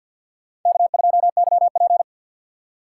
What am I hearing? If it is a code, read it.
Morse code, heard as a keyed tone, about 53 wpm, 700 Hz. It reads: K2YP
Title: USA Call Signs - 4 Characters - 6x Character Spacing - 50wpm